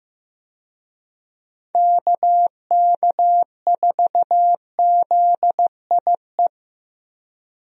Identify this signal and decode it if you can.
Morse code, heard as a keyed tone, about 15 wpm, 700 Hz. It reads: KK4ZIE
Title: USA Call Signs - 6 Characters - Repeated 3x - 15wpm